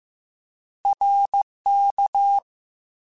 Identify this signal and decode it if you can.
Morse code, heard as a keyed tone, about 15 wpm, 800 Hz. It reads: RK
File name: Short Runs / Short Runs-mp3s / ShortContrasts @13